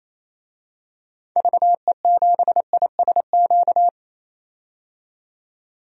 Morse code, heard as a keyed tone, about 28 wpm, 700 Hz. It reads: VE7ISQ
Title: DXCC-100 Call Signs - 6 Characters - 28wpm